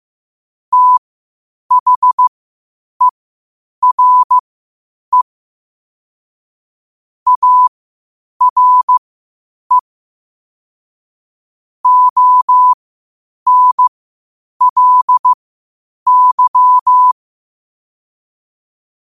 Morse code, heard as a keyed tone, about 15 wpm, spaced as for 8 wpm, 1000 Hz. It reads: THERE ARE ONLY
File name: qotdfiles / 8wpm